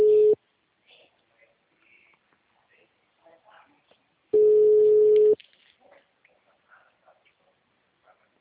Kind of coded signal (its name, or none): telephone call progress tone